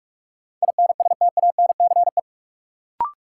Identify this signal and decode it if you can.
Morse code, heard as a keyed tone, about 42 wpm, 700 Hz. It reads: INSTANCE
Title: Sentences from Top 800 Words - New Word Drill - 45wpm